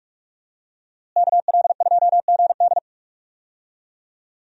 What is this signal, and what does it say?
Morse code, heard as a keyed tone, about 46 wpm, 700 Hz. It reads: KL2GD
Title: USA Call Signs - 5 Characters - Repeated 2x - 45wpm